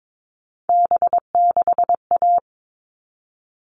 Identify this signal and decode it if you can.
Morse code, heard as a keyed tone, about 22 wpm, 700 Hz. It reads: B6A
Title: All Cut Numbers - 3-Digit 22wpm